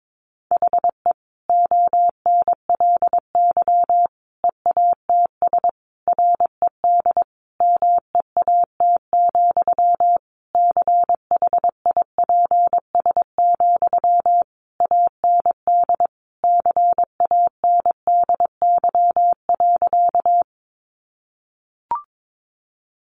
Morse code, heard as a keyed tone, about 22 wpm, 700 Hz. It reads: HE ONLY EATS RED MEAT, CHIPS, AND CANDY.
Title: Easy Phrases 22wpm